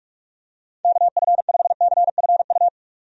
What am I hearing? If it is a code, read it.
Morse code, heard as a keyed tone, about 46 wpm, 700 Hz. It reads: KF5CFU
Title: USA Call Signs - 6 Characters - 45wpm